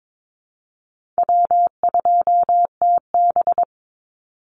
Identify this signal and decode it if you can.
Morse code, heard as a keyed tone, about 22 wpm, 700 Hz. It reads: W2TB
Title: Call Signs - 22wpm